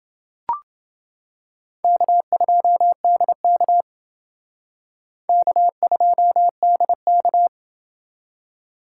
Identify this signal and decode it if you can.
Morse code, heard as a keyed tone, about 28 wpm, 700 Hz. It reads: K2DK K2DK